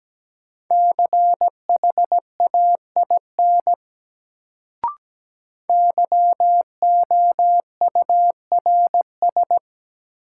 Morse code, heard as a keyed tone, about 17 wpm, 700 Hz. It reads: CHAIN YOURS
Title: Sentences from Top 1500 Words - New Word Drill Alt - 17wpm